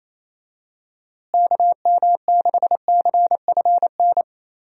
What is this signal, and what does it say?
Morse code, heard as a keyed tone, about 28 wpm, 700 Hz. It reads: KM6CFN
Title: USA Call Signs - 6 Characters - Repeated 2x - 28wpm